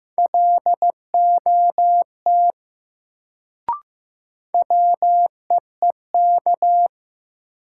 Morse code, heard as a keyed tone, about 15 wpm, 700 Hz. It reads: LOT WEEK
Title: Sentences from Top 500 Words - New Word Drill - 15wpm